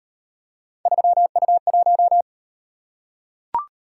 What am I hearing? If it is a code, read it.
Morse code, heard as a keyed tone, about 33 wpm, 700 Hz. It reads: 3U1